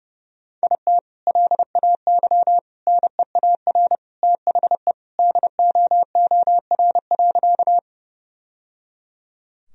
Morse code, heard as a keyed tone, about 30 wpm, 700 Hz. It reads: IT LAY NEAR THE DOOR.
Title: Sentences from Top 1000 Words - Review Part 2 - 30wpm